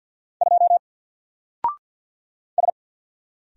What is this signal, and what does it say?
Morse code, heard as a keyed tone, about 50 wpm, 700 Hz. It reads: 2 S
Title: Single Character (Letter:Number:Punctuation:Prosign) 50wpm